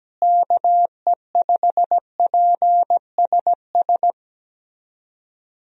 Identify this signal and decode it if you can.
Morse code, heard as a keyed tone, about 17 wpm, 700 Hz. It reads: KE5PSS
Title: USA Call Signs - 6 Characters - Repeated 3x - 17wpm